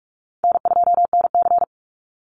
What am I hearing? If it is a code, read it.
Morse code, heard as a keyed tone, about 44 wpm, 700 Hz. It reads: N2NC
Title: Easy Call Signs Repeated 2x - 45wpm